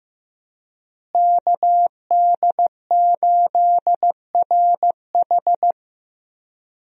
Morse code, heard as a keyed tone, about 15 wpm, 700 Hz. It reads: KD8RH
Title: USA Call Signs - 5 Characters - Repeated 2x - 15wpm